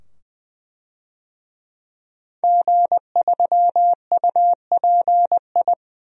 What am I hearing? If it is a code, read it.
Morse code, heard as a keyed tone, about 20 wpm, 700 Hz. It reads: G3UPI
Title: DXCC-100 Call Signs - 5 Characters - 20wpm